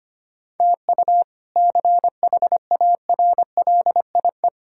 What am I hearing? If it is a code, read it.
Morse code, heard as a keyed tone, about 25 wpm, 700 Hz. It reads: TU CHARLIE
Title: K1USN SST QSO Practice - 25wpm